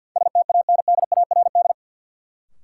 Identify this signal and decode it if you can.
Morse code, heard as a keyed tone, about 47 wpm, 700 Hz. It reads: STANDARD
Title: Sentences from Top 1000 Words - Review Part 6 - New Word Drill - 50wpm